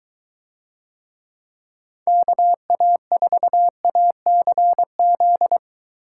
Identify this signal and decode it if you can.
Morse code, heard as a keyed tone, about 22 wpm, 700 Hz. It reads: KA4ACZ